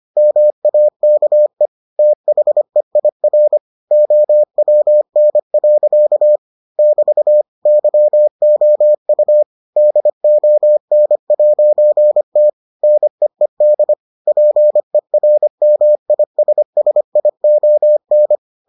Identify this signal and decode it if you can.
Morse code, heard as a keyed tone, about 25 wpm, 600 Hz. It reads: MAKE THEIR OWN. <BT> YOU DON'T NEED PERMISSION